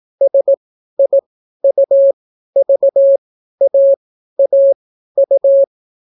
Morse code, heard as a keyed tone, about 17 wpm, spaced as for 12 wpm, 550 Hz. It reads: SIUVAAU